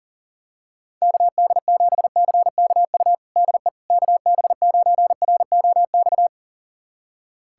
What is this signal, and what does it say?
Morse code, heard as a keyed tone, about 37 wpm, 700 Hz. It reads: KD7CKU DE KB9ROX
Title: USA Callsign DE Callsign 40wpm